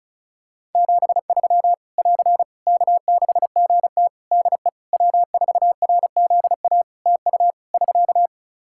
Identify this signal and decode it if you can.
Morse code, heard as a keyed tone, about 34 wpm, 700 Hz. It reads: 73 <AR> K6GT DE W4RZA TU <SK>
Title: Practice QSO - Exchange 4 - Ending - 35wpm